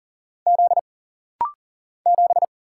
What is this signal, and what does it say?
Morse code, heard as a keyed tone, about 40 wpm, 700 Hz. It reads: Z 7
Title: Single Letter-Number - Rapid-Fire 40wpm